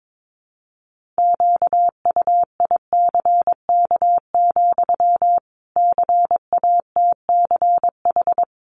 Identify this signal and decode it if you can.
Morse code, heard as a keyed tone, about 22 wpm, 700 Hz. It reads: QUICK, CATCH
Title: Sentences from Top 1500 Words 22wpm